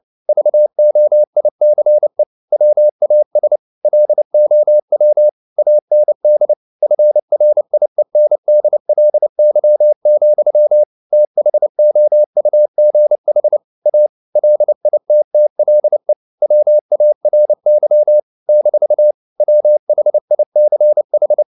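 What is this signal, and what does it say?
Morse code, heard as a keyed tone, about 29 wpm, 600 Hz. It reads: VOICE WAS LOW AND FRIENDLY, THOUGH A LITTLE WARY - WHICH
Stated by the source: Morse code practice at 28 wpm, text of Ken Follett's novel Never in English, from CW med Gnister.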